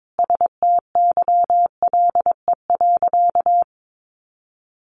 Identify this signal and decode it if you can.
Morse code, heard as a keyed tone, about 22 wpm, 700 Hz. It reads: STYLE.
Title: Sentences from Top 800 Words 22wpm